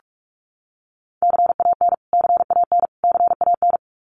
Morse code, heard as a keyed tone, about 44 wpm, 700 Hz. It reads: CAN CAN CAN
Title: Top 200 Words Repeated 45wpm